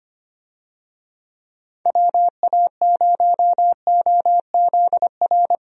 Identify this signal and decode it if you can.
Morse code, heard as a keyed tone, about 25 wpm, 700 Hz. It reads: WA0OZR
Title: USA Call Signs - 6 Characters - 25wpm